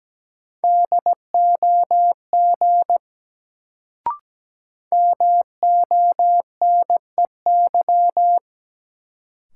Morse code, heard as a keyed tone, about 17 wpm, 700 Hz. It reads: DOG MONEY
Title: Sentences from Top 400 Words - New Word Drill - 17wpm